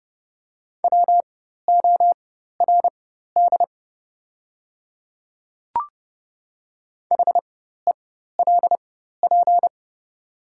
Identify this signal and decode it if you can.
Morse code, heard as a keyed tone, about 30 wpm, spaced as for 14 wpm, 700 Hz. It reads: WORD HELP
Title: 4x Character Spacing - Top 200 Words - 30wpm